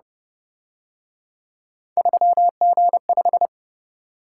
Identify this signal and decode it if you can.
Morse code, heard as a keyed tone, about 30 wpm, 700 Hz. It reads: 3G5